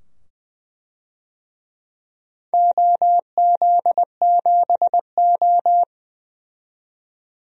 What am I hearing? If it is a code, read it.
Morse code, heard as a keyed tone, about 20 wpm, 700 Hz. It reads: OZ7O
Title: DXCC-100 Call Signs - 4 Characters - 20wpm